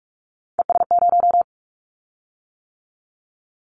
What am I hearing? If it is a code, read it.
Morse code, heard as a keyed tone, about 44 wpm, 700 Hz. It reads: ES0
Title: Callsign Prefixes for Top 100 DXCC - 45wpm